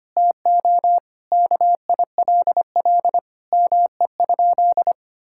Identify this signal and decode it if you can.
Morse code, heard as a keyed tone, about 25 wpm, 700 Hz. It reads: TO KILL ME?
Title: Sentences from Top 1600 Words 25wpm